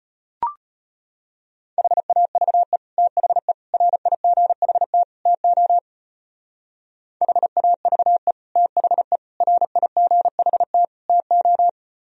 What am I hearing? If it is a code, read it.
Morse code, heard as a keyed tone, about 33 wpm, 700 Hz. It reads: HAVE THE RIGHT TO HAVE THE RIGHT TO